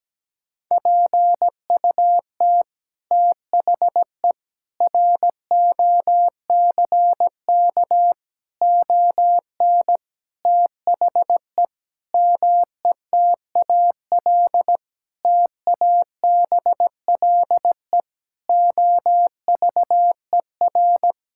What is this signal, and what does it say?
Morse code, heard as a keyed tone, about 17 wpm, 700 Hz. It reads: PUT THE ROCK ON THE METAL TABLE OVER
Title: Sentences from Top 1000 Words - Review Part 6 - 17wpm